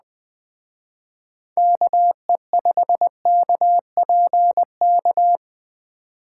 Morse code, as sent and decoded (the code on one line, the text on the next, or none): -.- . ..... -.- .--. -.-
KE5KPK